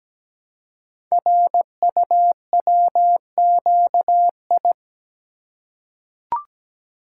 Morse code, heard as a keyed tone, about 17 wpm, 700 Hz. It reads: RUWQI
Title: Code Groups - 5 Letters - 17wpm